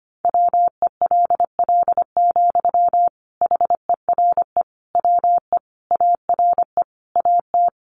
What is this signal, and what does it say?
Morse code, heard as a keyed tone, about 25 wpm, 700 Hz. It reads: WELL, HERE WE ARE AT
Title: Sentences from Top 300 Words 25wpm